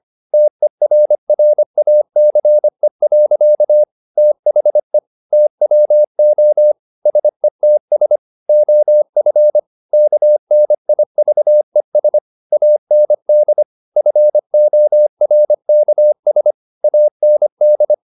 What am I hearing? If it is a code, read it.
Morse code, heard as a keyed tone, about 25 wpm, 600 Hz. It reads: TERRACE. THE TWO SETS OF KNIVES AND FORKS AND
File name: the old man and the sea part 1 25wpm